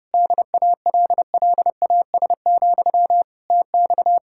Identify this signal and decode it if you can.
Morse code, heard as a keyed tone, about 33 wpm, 700 Hz. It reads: DALLAS, TX